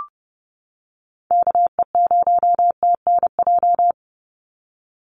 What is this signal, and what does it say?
Morse code, heard as a keyed tone, about 30 wpm, 700 Hz. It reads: KE0TNJ
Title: USA Call Signs 30wpm